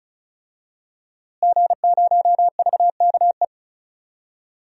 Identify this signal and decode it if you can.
Morse code, heard as a keyed tone, about 34 wpm, 700 Hz. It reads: G0VKE
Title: DXCC-100 Call Signs - 5 Characters - Repeated 3x - 35wpm